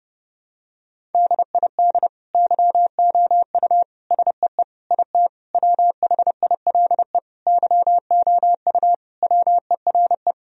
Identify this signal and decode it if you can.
Morse code, heard as a keyed tone, about 30 wpm, 700 Hz. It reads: DID YOU SEE IT WHILE YOU WERE
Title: Sentences from Top 1000 Words - Review Part 1 - 30wpm